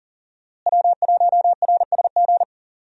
Morse code, heard as a keyed tone, about 37 wpm, 700 Hz. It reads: W1RSG
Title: USA Call Signs - 5 Characters - Repeated 3x - 40wpm